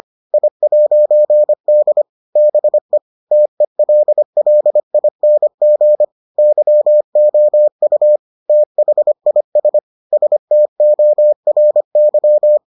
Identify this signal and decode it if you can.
Morse code, heard as a keyed tone, about 25 wpm, 600 Hz. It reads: I'D BE TELLING YOU THIS STORY